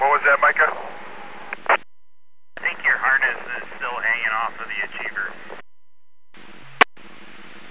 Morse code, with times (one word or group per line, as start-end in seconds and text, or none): none